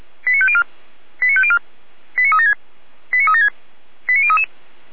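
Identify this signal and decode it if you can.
Five-tone selective call (ZVEI), heard as quick runs of five tones, tones of about 70 ms. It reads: 89504 89504 89376 89376 89033
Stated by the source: ZVEI five-tone call received by SDR.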